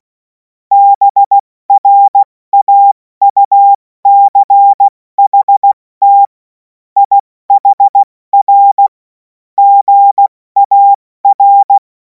Morse code, heard as a keyed tone, about 16 wpm, 800 Hz. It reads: BRAUCHT IHR GAR
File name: Morsekurs / Witze / Witz 10. Februar 2026 70 F80